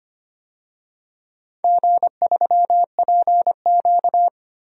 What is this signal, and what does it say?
Morse code, heard as a keyed tone, about 25 wpm, 700 Hz. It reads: G3PQ
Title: DXCC-100 Call Signs - 4 Characters - 25wpm